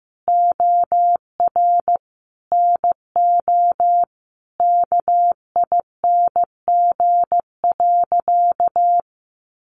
Morse code, heard as a keyed tone, about 15 wpm, 700 Hz. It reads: OR NO KING.
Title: Sentences from Top 600 Words 15wpm